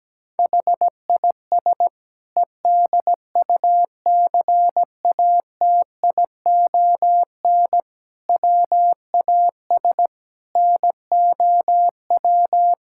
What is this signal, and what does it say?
Morse code, heard as a keyed tone, about 17 wpm, 700 Hz. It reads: HIS EDUCATION WAS NOW
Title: Sentences from Top 1000 Words - Review Part 6 - 17wpm